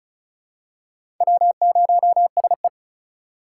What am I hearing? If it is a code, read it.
Morse code, heard as a keyed tone, about 36 wpm, 700 Hz. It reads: W0SE
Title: USA Call Signs - 4 Characters - 4x Character Spacing - 35wpm